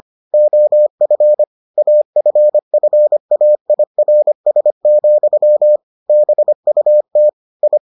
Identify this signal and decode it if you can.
Morse code, heard as a keyed tone, about 25 wpm, 600 Hz. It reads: OF AFFAIRS, BUT I